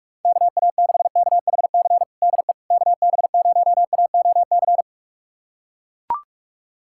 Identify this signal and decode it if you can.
Morse code, heard as a keyed tone, about 41 wpm, 700 Hz. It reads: KA6KHC DE KB0AOC
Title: USA Callsign DE Callsign - 6 Characters - 45wpm